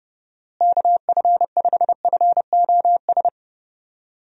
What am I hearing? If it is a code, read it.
Morse code, heard as a keyed tone, about 32 wpm, 700 Hz. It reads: KF5FOS
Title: USA Call Signs - 6 Characters - Repeated 3x - 30wpm